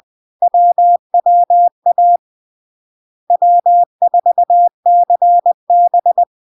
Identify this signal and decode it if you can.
Morse code, heard as a keyed tone, about 20 wpm, 700 Hz. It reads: WWA W4CB
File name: LARK-FD-2022-Log-in-CW-20wpm-700-Hz